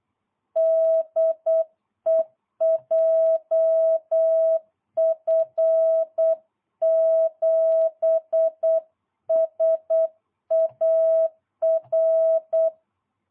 Morse code, heard as a keyed tone, about 8 wpm, 660 Hz. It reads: DE JF7SAR